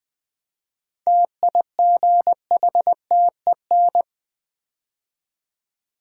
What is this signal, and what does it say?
Morse code, heard as a keyed tone, about 21 wpm, 700 Hz. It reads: TIGHTEN
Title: 7 Letter Words - First 3 Letters Given - 20wpm